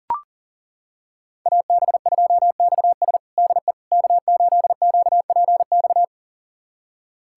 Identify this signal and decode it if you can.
Morse code, heard as a keyed tone, about 40 wpm, 700 Hz. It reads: AB2XS DE K8QPX
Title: USA Callsign DE Callsign - 5 Characters - 40wpm